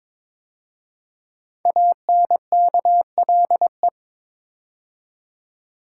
Morse code, heard as a keyed tone, about 22 wpm, 700 Hz. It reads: ANKLE
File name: 5 Letter Words - First 3 Letters Given - 22wpm